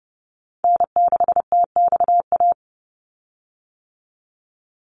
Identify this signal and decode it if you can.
Morse code, heard as a keyed tone, about 30 wpm, 700 Hz. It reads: N6TXA
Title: USA Call Signs 30wpm